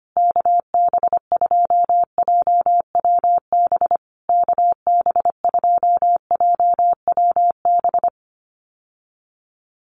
Morse code, heard as a keyed tone, about 25 wpm, 700 Hz. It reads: KB2JWB KB2JWB
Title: USA Call Signs Repeated 3x 25wpm